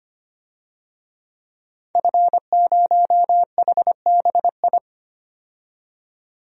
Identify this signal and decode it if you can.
Morse code, heard as a keyed tone, about 25 wpm, 700 Hz. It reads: F0HBI
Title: DXCC-100 Call Signs - 5 Characters - 25wpm